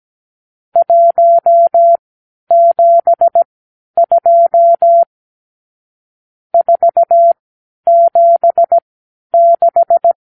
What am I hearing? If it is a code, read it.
Morse code, heard as a keyed tone, about 17 wpm, spaced as for 10 wpm, 680 Hz. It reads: 172 476
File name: k7qo / Lesson 54 (9)